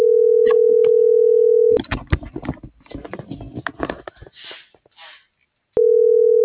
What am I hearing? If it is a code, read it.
Ringback tone (North America), 440 + 480 Hz.